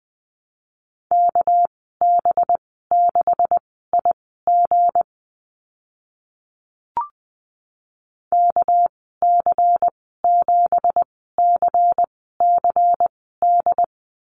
Morse code, heard as a keyed tone, about 20 wpm, spaced as for 14 wpm, 700 Hz. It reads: KB6IG KC7CCD
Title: 200 Lesson 32g - Letter k - Call Signs 20wpm